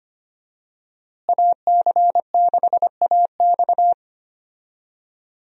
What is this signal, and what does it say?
Morse code, heard as a keyed tone, about 25 wpm, 700 Hz. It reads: AC6AX
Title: USA Call Signs - 5 Characters - Repeated 3x - 25wpm